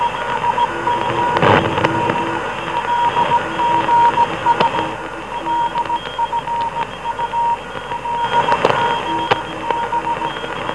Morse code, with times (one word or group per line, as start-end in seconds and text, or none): none